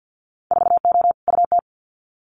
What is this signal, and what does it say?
Morse code, heard as a keyed tone, about 50 wpm, 700 Hz. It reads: 4O UT
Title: Winter Field Day Exchanges Encore 50wpm